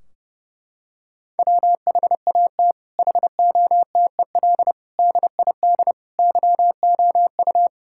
Morse code, heard as a keyed tone, about 31 wpm, 700 Hz. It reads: WHAT HOTEL DID YOU
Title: Sentences from Top 1600 Words 30wpm